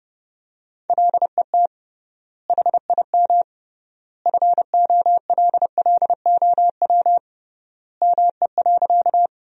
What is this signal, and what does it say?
Morse code, heard as a keyed tone, about 30 wpm, 700 Hz. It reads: LET HIM FOLLOW ME.
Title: 3x Word-Spacing - Sentences from Top 300 Words - 30wpm